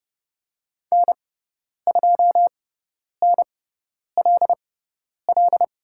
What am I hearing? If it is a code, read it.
Morse code, heard as a keyed tone, about 30 wpm, spaced as for 10 wpm, 700 Hz. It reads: N2NLL
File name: Lesson 18g - Letter m - Call Signs 30wpm